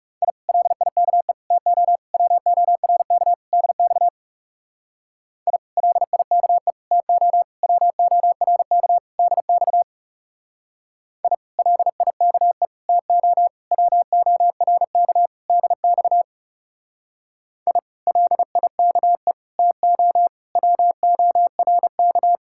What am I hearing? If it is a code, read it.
Morse code, heard as a keyed tone, about 40 wpm, 700 Hz. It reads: I LIKE TO WORK DX I LIKE TO WORK DX I LIKE TO WORK DX I LIKE TO WORK